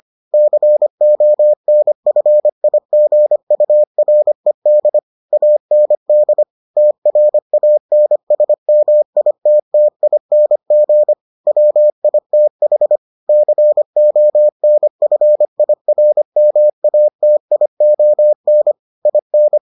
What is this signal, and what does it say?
Morse code, heard as a keyed tone, about 25 wpm, 600 Hz. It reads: CONFIGURED AND TRANSMITTING WITH CONFIRMATION IN